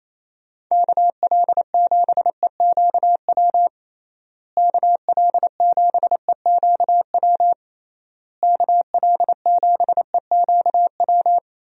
Morse code, heard as a keyed tone, about 28 wpm, 700 Hz. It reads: KL7EQW KL7EQW KL7EQW
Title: DXCC-100 Call Signs - 6 Characters - Repeated 3x - 28wpm